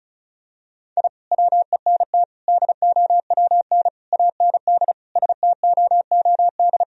Morse code, heard as a keyed tone, about 36 wpm, 700 Hz. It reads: I WENT DOWN AND STOOD
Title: Sentences from Top 1000 Words - Review Part 2 - 35wpm